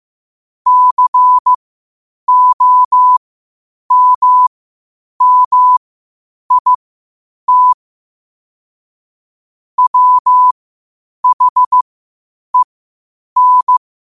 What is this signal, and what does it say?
Morse code, heard as a keyed tone, about 15 wpm, spaced as for 8 wpm, 1000 Hz. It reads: COMMIT WHEN
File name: 8wpm